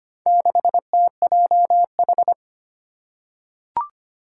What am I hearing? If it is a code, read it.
Morse code, heard as a keyed tone, about 25 wpm, 700 Hz. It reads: 6TJH
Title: Four Challenging Letter-Number Combinations 25wpm